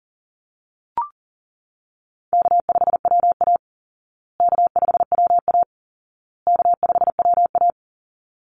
Morse code, heard as a keyed tone, about 40 wpm, 700 Hz. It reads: K5WA K5WA K5WA
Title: Call Signs Repeated 2x - 40wpm